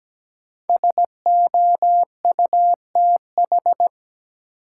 Morse code, heard as a keyed tone, about 17 wpm, 700 Hz. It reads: SOUTH
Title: Sentences from Top 400 Words - New Word Drill - 17wpm